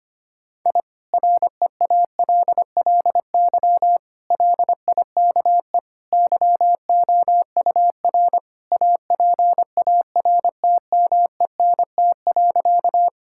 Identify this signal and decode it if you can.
Morse code, heard as a keyed tone, about 25 wpm, 700 Hz. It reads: I REALLY LIKE YOUR APARTMENT.
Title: Sentences from Top 1000 Words - Review Part 6 - 25wpm